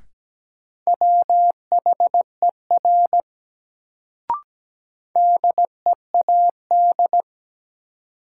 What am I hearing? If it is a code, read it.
Morse code, heard as a keyed tone, about 17 wpm, 700 Hz. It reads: WHER DEAD
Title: Sentences from Top 1500 Words - New Word Drill - 17wpm